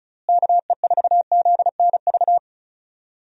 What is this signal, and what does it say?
Morse code, heard as a keyed tone, about 36 wpm, 700 Hz. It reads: KE4ZNV
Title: USA Call Signs - 6 Characters - Repeated 3x - 35wpm